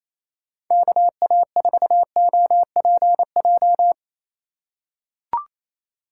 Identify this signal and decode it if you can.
Morse code, heard as a keyed tone, about 28 wpm, 700 Hz. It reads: KA4OPJ
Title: USA Call Signs - 6 Characters - 3x Character Spacing - 28wpm